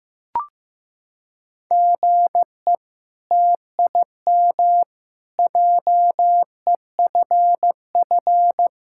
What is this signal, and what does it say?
Morse code, heard as a keyed tone, about 15 wpm, 700 Hz. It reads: GE TIM JEFF
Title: K1USN SST QSO Practice - 15wpm